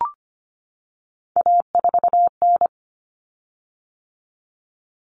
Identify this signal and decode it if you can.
Morse code, heard as a keyed tone, about 25 wpm, 700 Hz. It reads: A4N
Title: Common Cut Numbers - 3-Digit 25wpm